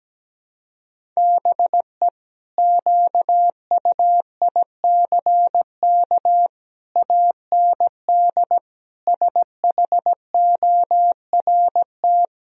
Morse code, heard as a keyed tone, about 17 wpm, 700 Hz. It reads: BE QUICK AND SHORT
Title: Sentences from Top 1000 Words - Review Part 2 - 17wpm